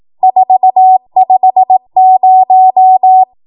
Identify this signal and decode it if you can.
Morse code, heard as a keyed tone, about 18 wpm, 750 Hz. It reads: 450
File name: Morse / 18WPM